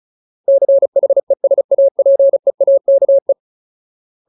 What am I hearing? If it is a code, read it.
Morse code, heard as a keyed tone, about 36 wpm, 550 Hz. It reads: CHESAPEAKE
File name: Ultimate Word List / 35WPM Ultimate